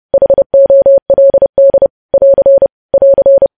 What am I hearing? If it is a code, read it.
Morse code, heard as a keyed tone, about 30 wpm, 550 Hz. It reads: HOLD <AR> <AR>
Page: news headlines in morse code at 30 wpm